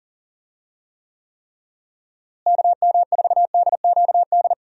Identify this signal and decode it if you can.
Morse code, heard as a keyed tone, about 37 wpm, 700 Hz. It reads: KM4DQD
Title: USA Call Signs - 6 Characters - Repeated 3x - 40wpm